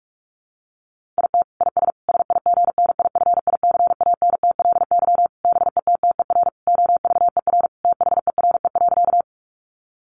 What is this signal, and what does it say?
Morse code, heard as a keyed tone, about 44 wpm, 700 Hz. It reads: IT IS SIGNIFICANTLY BETTER OVER THERE.
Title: Sentences from Top 1500 Words 45wpm